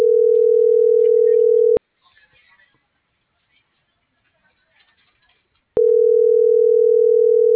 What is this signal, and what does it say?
Ringback tone (North America), 440 + 480 Hz.
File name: CENTRAL SAGOT / out xquenda.sagot - .83565